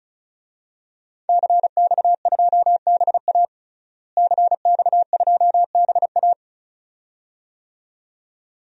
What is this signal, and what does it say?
Morse code, heard as a keyed tone, about 34 wpm, 700 Hz. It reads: CX2BA CX2BA
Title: DXCC-100 Call Signs - 5 Characters - Repeated 2x - 35wpm